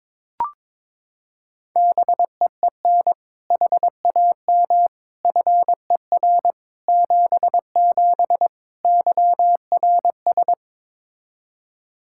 Morse code, heard as a keyed tone, about 21 wpm, 700 Hz. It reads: BEEN HAM FER 77 YRS
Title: Practice QSO - Exchange 3 - More Info - 22wpm